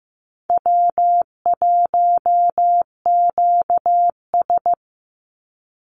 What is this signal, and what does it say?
Morse code, heard as a keyed tone, about 14 wpm, 700 Hz. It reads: W1QS
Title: Call Signs Repeated 3x - 15wpm